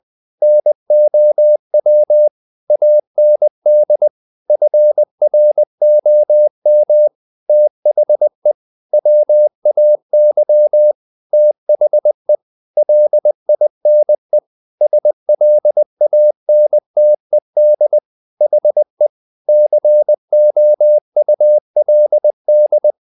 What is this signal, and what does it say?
Morse code, heard as a keyed tone, about 20 wpm, 600 Hz. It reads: NOW AND FROM THE WAY THE LINE SLANTED HE COULD
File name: the old man and the sea part 2 20wpm